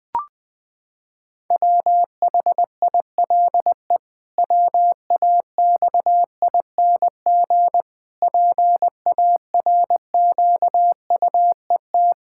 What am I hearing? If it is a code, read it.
Morse code, heard as a keyed tone, about 21 wpm, 700 Hz. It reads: WHILE WAXING PARQUET
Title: Pangram Challenge v2 - 20wpm